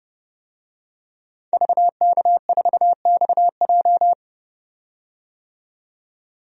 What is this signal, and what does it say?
Morse code, heard as a keyed tone, about 30 wpm, 700 Hz. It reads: VK4XJ
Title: DXCC-100 Call Signs - 5 Characters - 30wpm